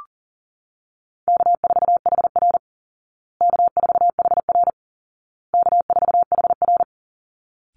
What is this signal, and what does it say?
Morse code, heard as a keyed tone, about 40 wpm, 700 Hz. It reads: K4HR K4HR K4HR
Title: Call Signs Repeated 2x - 40wpm